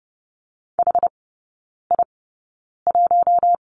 Morse code, heard as a keyed tone, about 30 wpm, spaced as for 9 wpm, 700 Hz. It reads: HI1
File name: 081 Lesson 14e - Number 2 - Three Characters 30wpm